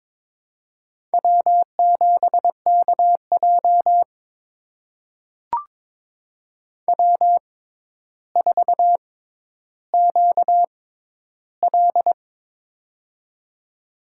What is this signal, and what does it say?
Morse code, heard as a keyed tone, about 22 wpm, spaced as for 11 wpm, 700 Hz. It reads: W7KJ W4QL
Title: USA Call Signs - 6x Character Spacing - 22wpm